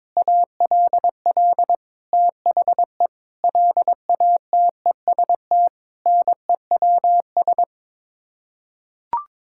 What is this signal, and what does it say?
Morse code, heard as a keyed tone, about 22 wpm, 700 Hz. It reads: ALL THE LATEST NEWS
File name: Sets of 9 Words - 22wpm